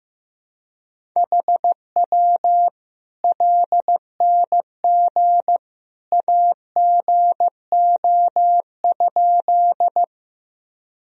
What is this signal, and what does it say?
Morse code, heard as a keyed tone, about 15 wpm, 700 Hz. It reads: HW LNG AGO?
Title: Sentences from Top 500 Words and Abbreviations 15wpm